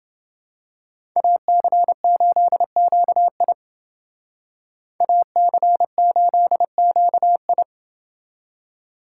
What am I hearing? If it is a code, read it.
Morse code, heard as a keyed tone, about 29 wpm, 700 Hz. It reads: AC8QI AC8QI